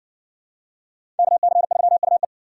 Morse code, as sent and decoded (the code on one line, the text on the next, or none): -.- -.-. ...-- ..- .
KC3UE